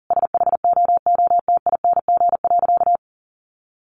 Morse code, heard as a keyed tone, about 40 wpm, 700 Hz. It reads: SHOOTING.
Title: A Sound Of Thunder 40wpm